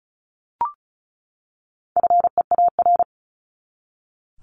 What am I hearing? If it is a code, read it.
Morse code, heard as a keyed tone, about 34 wpm, 700 Hz. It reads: FEAR
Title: Top 1751-2000 Words 35wpm